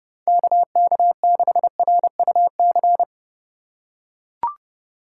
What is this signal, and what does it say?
Morse code, heard as a keyed tone, about 29 wpm, 700 Hz. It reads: KK6RUC